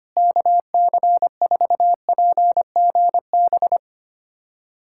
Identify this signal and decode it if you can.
Morse code, heard as a keyed tone, about 25 wpm, 700 Hz. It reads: KC4PGB
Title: USA Callsign DE Callsign 25wpm